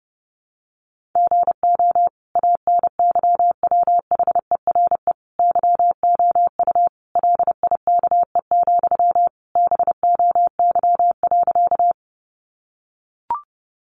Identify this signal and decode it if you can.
Morse code, heard as a keyed tone, about 30 wpm, 700 Hz. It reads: GO ANYWHERE YOU LIKE, BOY.
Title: Sentences from Top 1400 Words 30wpm